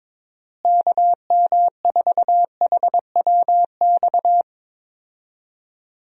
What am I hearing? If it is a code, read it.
Morse code, heard as a keyed tone, about 22 wpm, 700 Hz. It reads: KM4HWX
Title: USA Call Signs - 6 Characters - Repeated 2x - 22wpm